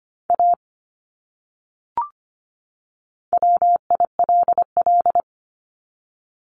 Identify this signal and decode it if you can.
Morse code, heard as a keyed tone, about 25 wpm, 700 Hz. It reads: A WILL
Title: Top 200 Words 25wpm